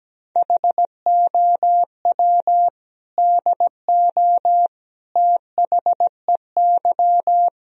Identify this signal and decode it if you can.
Morse code, heard as a keyed tone, about 17 wpm, 700 Hz. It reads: HOW DO THEY
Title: Sentences from Top 1000 Words - Review Part 2 - 17wpm